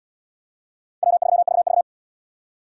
Morse code, heard as a keyed tone, about 68 wpm, 700 Hz. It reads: W3VV